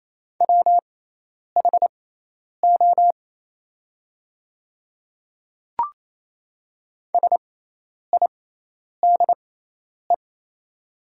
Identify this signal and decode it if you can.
Morse code, heard as a keyed tone, about 27 wpm, spaced as for 10 wpm, 690 Hz. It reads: WHO SIDE
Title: 6x Character Spacing - Top 500 Words - 28wpm